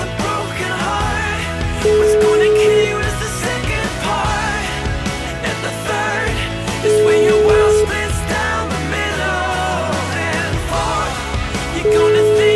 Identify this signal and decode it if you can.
Ringback tone (Europe, 425 Hz), with noise.